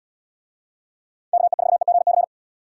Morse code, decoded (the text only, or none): KE4EGC